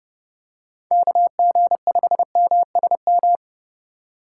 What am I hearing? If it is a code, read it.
Morse code, heard as a keyed tone, about 29 wpm, 700 Hz. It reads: KG5MSM
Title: USA Call Signs - 6 Characters - 3x Character Spacing - 30wpm